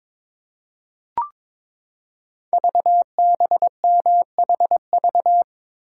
Morse code, heard as a keyed tone, about 22 wpm, 700 Hz. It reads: VBMHV
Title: Code Groups - 5 Letters - 22wpm